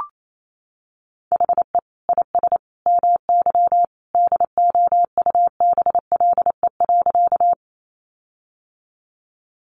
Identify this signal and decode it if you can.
Morse code, heard as a keyed tone, about 28 wpm, 700 Hz. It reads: HE IS MY DOUBLE.
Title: Sentences from Top 1400 Words 28wpm